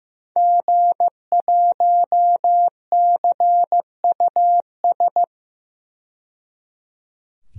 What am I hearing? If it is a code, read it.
Morse code, heard as a keyed tone, about 15 wpm, 700 Hz. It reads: G1CUS
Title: DXCC-100 Call Signs - 5 Characters - 15wpm